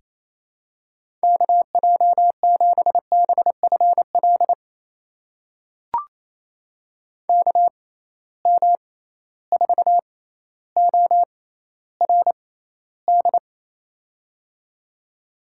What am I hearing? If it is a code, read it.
Morse code, heard as a keyed tone, about 28 wpm, spaced as for 14 wpm, 700 Hz. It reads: KJ7BFL KM4ORD
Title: USA Call Signs - 6 Characters - 6x Character Spacing - 28wpm